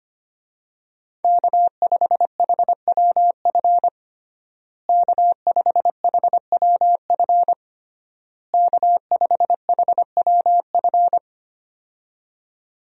Morse code, heard as a keyed tone, about 25 wpm, 700 Hz. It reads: K5HWF K5HWF K5HWF
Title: USA Call Signs - 5 Characters - Repeated 3x - 25wpm